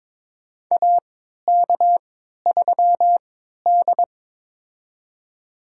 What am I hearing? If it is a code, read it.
Morse code, heard as a keyed tone, about 22 wpm, spaced as for 12 wpm, 700 Hz. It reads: AK3D
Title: USA Call Signs - 3x Character Spacing - 22wpm